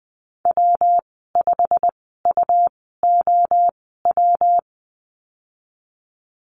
Morse code, heard as a keyed tone, about 19 wpm, spaced as for 14 wpm, 700 Hz. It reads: W5UOW
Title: 186 Lesson 30g - Letter b - Call Signs 20wpm